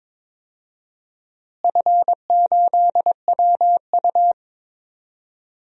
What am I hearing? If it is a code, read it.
Morse code, heard as a keyed tone, about 22 wpm, 700 Hz. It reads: F8WU
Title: DXCC-100 Call Signs - 4 Characters - Repeated 3x - 22wpm